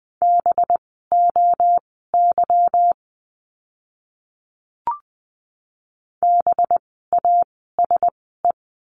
Morse code, heard as a keyed tone, about 20 wpm, spaced as for 14 wpm, 700 Hz. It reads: BOY BASE
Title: 185 Lesson 30f - Letter b - Words 20wpm